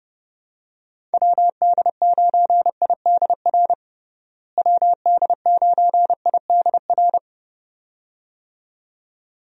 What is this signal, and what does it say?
Morse code, heard as a keyed tone, about 32 wpm, 700 Hz. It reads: WD9IDR WD9IDR
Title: USA Call Signs - 6 Characters - Repeated 2x - 30wpm